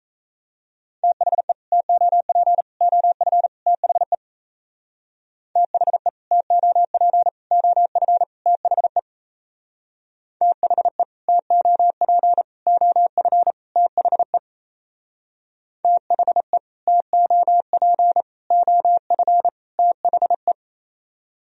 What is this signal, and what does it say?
Morse code, heard as a keyed tone, about 35 wpm, 700 Hz. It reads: THE TOP OF THE THE TOP OF THE THE TOP OF THE THE TOP OF THE
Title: Speed-Racing - Sets of 4 Words 28wpm